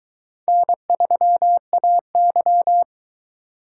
Morse code, heard as a keyed tone, about 22 wpm, 700 Hz. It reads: N3AY